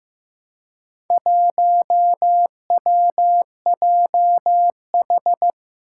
Morse code, heard as a keyed tone, about 15 wpm, 700 Hz. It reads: 1WJH
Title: Four Challenging Letter-Number Combinations 15wpm